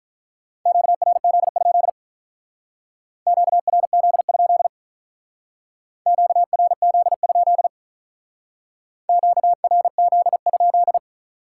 Morse code, heard as a keyed tone, about 44 wpm, 700 Hz. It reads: QRZ? QRZ? QRZ? QRZ?
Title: Speed-Racing Common QSO Phrases Encore - 35wpm